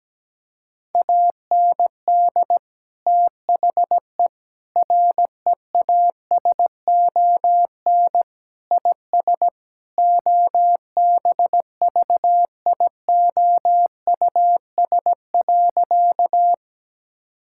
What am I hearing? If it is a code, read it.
Morse code, heard as a keyed tone, about 17 wpm, 700 Hz. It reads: AND THE REASON IS OBVIOUS.
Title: Sentences from Top 1600 Words 17wpm